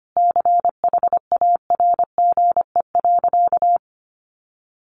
Morse code, heard as a keyed tone, about 25 wpm, 700 Hz. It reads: CHARGE.
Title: Sentences from Top 900 Words 25wpm